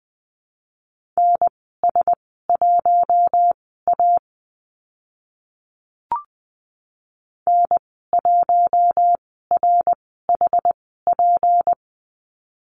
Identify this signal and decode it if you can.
Morse code, heard as a keyed tone, about 20 wpm, spaced as for 15 wpm, 700 Hz. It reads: NS1A N1RHP
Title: 167 Lesson 27g - Number 7 - Call Signs 20wpm